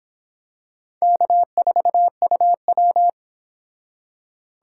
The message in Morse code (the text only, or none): K4UW